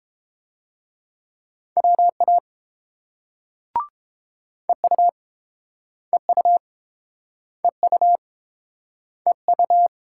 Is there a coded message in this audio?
Yes: Morse code from 1.8 to 9.9 s, heard as a keyed tone, about 30 wpm, 700 Hz. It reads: WA EU EU EU EU